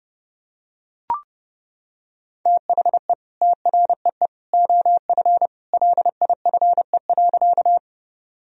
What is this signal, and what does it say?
Morse code, heard as a keyed tone, about 32 wpm, 700 Hz. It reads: THE TREE OF LIFE.
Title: Sentences from Top 1000 Words - Review Part 1 - 30wpm